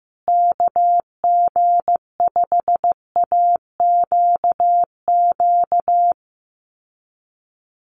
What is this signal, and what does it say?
Morse code, heard as a keyed tone, about 15 wpm, 700 Hz. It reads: KG5AQQ
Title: USA Call Signs Repeated 3x 15wpm